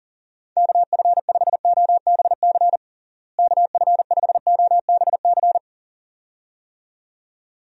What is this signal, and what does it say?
Morse code, heard as a keyed tone, about 40 wpm, 700 Hz. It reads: KF5OBC KF5OBC
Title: USA Call Signs - 6 Characters - Repeated 2x - 40wpm